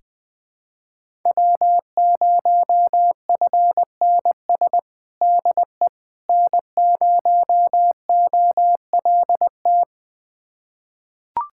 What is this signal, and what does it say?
Morse code, heard as a keyed tone, about 20 wpm, 700 Hz. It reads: W0FNS DE N0OLT
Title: USA Callsign DE Callsign - 5 Characters - 20wpm